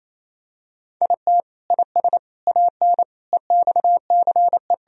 Morse code, heard as a keyed tone, about 28 wpm, 700 Hz. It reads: IT IS AN EXCE
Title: Sentences from Top 1000 Words - Review Part 2 - 28wpm